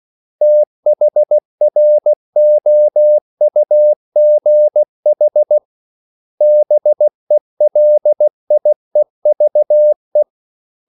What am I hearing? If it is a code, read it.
Morse code, heard as a keyed tone, about 16 wpm, 600 Hz. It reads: THROUGH BELIEVE
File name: CW / Practice / 500MostCommonEnglishWords16